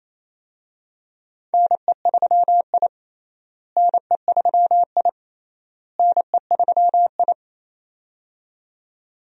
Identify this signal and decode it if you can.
Morse code, heard as a keyed tone, about 28 wpm, 700 Hz. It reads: NE3I NE3I NE3I
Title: USA Call Signs - 4 Characters - Repeated 3x - 28wpm